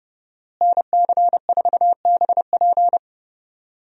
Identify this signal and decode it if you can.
Morse code, heard as a keyed tone, about 30 wpm, 700 Hz. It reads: NC4BP
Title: USA Call Signs - 5 Characters - Repeated 2x - 30wpm